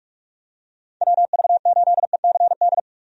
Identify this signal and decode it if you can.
Morse code, heard as a keyed tone, about 45 wpm, 700 Hz. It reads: WV8ECD
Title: USA Call Signs - 6 Characters - 45wpm